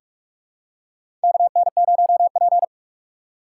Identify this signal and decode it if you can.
Morse code, heard as a keyed tone, about 46 wpm, 700 Hz. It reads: KN0P
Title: USA Call Signs - 4 Characters - 4x Character Spacing - 45wpm